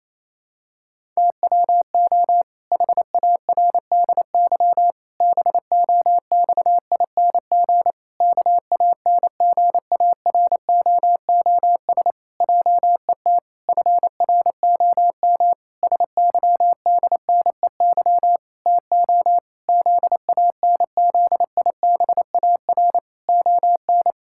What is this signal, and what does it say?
Morse code, heard as a keyed tone, about 28 wpm, 700 Hz. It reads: TWO HARDY BOXING KANGAROOS JET FROM SYDNEY TO ZANZIBAR ON
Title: Pangram Challenge v2 - 28wpm